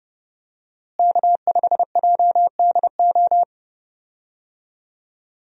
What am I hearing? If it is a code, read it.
Morse code, heard as a keyed tone, about 30 wpm, 700 Hz. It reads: K5JDO
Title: USA Call Signs - 5 Characters - Repeated 3x - 30wpm